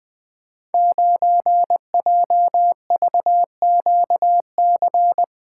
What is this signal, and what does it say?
Morse code, heard as a keyed tone, about 21 wpm, 700 Hz. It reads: 9JVQC
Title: Code Groups - 5 Letter:Numbers - 20wpm